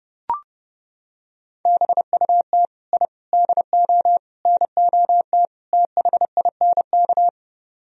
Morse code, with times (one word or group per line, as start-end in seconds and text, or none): 1.7-2.7 BUT
2.9-3.1 I
3.3-4.2 DO
4.5-5.5 NOT
5.7-7.3 THINK